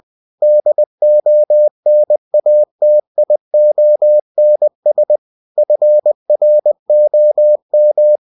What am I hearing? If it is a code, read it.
Morse code, heard as a keyed tone, about 20 wpm, 600 Hz. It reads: DONATIONS FROM